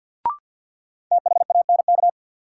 Morse code, heard as a keyed tone, about 53 wpm, 700 Hz. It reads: THANK
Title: Sentences from Top 600 Words - New Word Drill - 50wpm